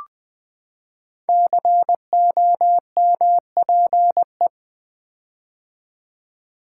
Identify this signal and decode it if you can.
Morse code, heard as a keyed tone, about 20 wpm, 700 Hz. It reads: COMPE
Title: Build A Word - 5 Letters At A Time - 10-Letter Words - 20wpm